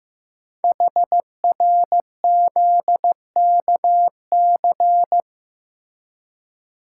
Morse code, heard as a keyed tone, about 15 wpm, 700 Hz. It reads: HRZKC
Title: Code Groups - 5 Letters - 15wpm